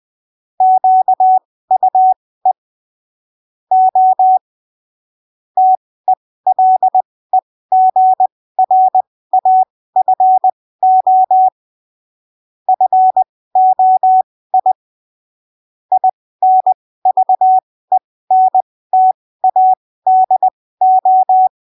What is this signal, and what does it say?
Morse code, heard as a keyed tone, about 20 wpm, spaced as for 15 wpm, 750 Hz. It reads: QUE O TELEGRAFO FOI INVENTADO
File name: arquivos / texto / texto 17 20ppm